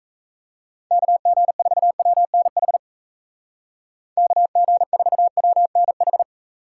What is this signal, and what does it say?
Morse code, heard as a keyed tone, about 36 wpm, 700 Hz. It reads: KG4WNH KG4WNH